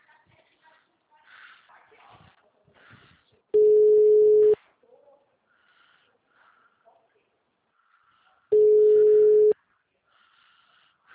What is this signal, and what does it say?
Ringback tone (Europe, 425 Hz).
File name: out- -2007- - - .1282